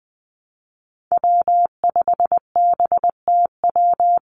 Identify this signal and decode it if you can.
Morse code, heard as a keyed tone, about 20 wpm, 700 Hz. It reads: W5BTW
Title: USA Call Signs 20wpm